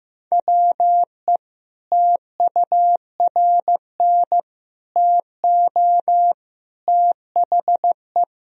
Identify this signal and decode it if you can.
Morse code, heard as a keyed tone, about 15 wpm, 700 Hz. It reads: WE TURN TO THE
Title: Sentences from Top 1000 Words - Review Part 1 - 15wpm